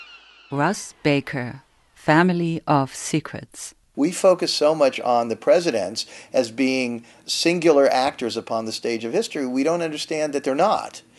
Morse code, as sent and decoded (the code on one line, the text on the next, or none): none